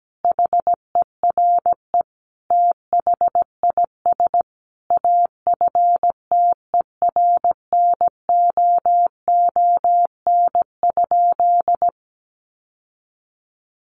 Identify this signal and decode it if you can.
Morse code, heard as a keyed tone, about 17 wpm, 700 Hz. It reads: HERE THIS AFTERNOON?
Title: Sentences from Top 1500 Words 17wpm